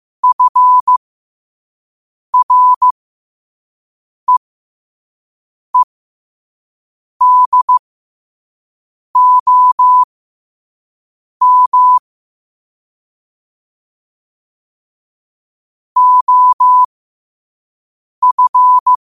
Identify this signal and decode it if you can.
Morse code, heard as a keyed tone, about 15 wpm, spaced as for 5 wpm, 1000 Hz. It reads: FREEDOM OF
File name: qotdfiles / 5wpm